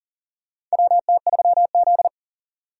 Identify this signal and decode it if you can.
Morse code, heard as a keyed tone, about 40 wpm, 700 Hz. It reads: WT3Z
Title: USA Call Signs - 4 Characters - 4x Character Spacing - 40wpm